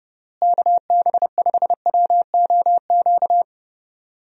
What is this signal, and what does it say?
Morse code, heard as a keyed tone, about 30 wpm, 700 Hz. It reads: KB5WOQ
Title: USA Call Signs - 6 Characters - Repeated 3x - 30wpm